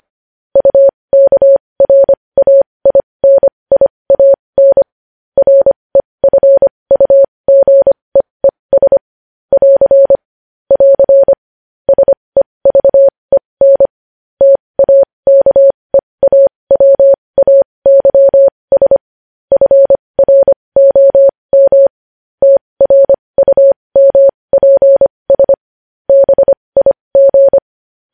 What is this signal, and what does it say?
Morse code, heard as a keyed tone, about 25 wpm, spaced as for 20 wpm, 550 Hz. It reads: UKRAINIAN REFUGEES <AR> <AR> SEVEN TAKEAWAYS FROM TRUMPS BIG
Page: news-headlines-in-morse-code-at-20-wpm